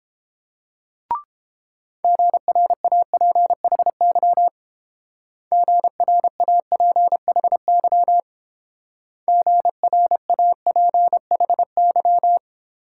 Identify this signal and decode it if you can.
Morse code, heard as a keyed tone, about 27 wpm, 700 Hz. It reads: GRAPHY GRAPHY GRAPHY
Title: Speed-Racing Multi-letter Phonemes - 22wpm